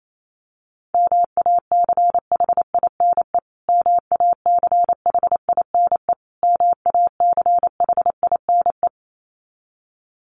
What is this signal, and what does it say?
Morse code, heard as a keyed tone, about 28 wpm, 700 Hz. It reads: MACHINE MACHINE MACHINE
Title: Top 500 Words Repeated 28wpm